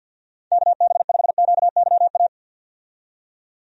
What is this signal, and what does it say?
Morse code, heard as a keyed tone, about 51 wpm, 700 Hz. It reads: KB5QYA
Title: USA Call Signs - 6 Characters - 50wpm